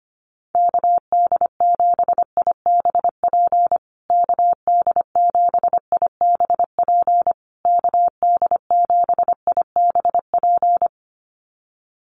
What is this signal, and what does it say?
Morse code, heard as a keyed tone, about 25 wpm, 700 Hz. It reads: KD7IBP KD7IBP KD7IBP
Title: USA Call Signs Repeated 3x 25wpm